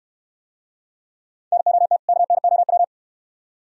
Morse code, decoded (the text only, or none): NOT KNOW